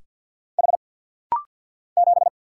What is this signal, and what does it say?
Morse code, heard as a keyed tone, about 50 wpm, 700 Hz. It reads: H 7
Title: Single Letter-Number Alt - Rapid-Fire 50wpm